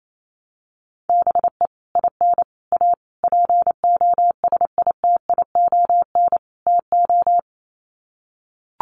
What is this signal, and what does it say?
Morse code, heard as a keyed tone, about 28 wpm, 700 Hz. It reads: BE IN A POSITION TO
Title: Sets of 5 Words Encore2 28wpm